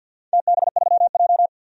Morse code, heard as a keyed tone, about 53 wpm, 700 Hz. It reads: TB3J
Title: DXCC-100 Call Signs - 4 Characters - 50wpm